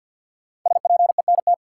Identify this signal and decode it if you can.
Morse code, heard as a keyed tone, about 45 wpm, 700 Hz. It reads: SPENT
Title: Sets of 5 Words - New Word Drill - 50wpm